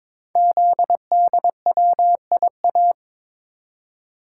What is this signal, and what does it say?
Morse code, heard as a keyed tone, about 22 wpm, 700 Hz. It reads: ZDWIA
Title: Code Groups - 5 Letters - 22wpm